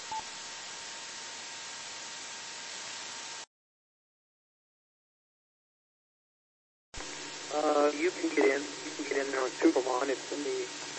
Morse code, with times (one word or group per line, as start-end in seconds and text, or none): none